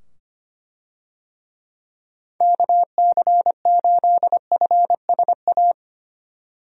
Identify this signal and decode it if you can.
Morse code, heard as a keyed tone, about 25 wpm, 700 Hz. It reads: KC8FSA
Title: USA Call Signs - 6 Characters - 25wpm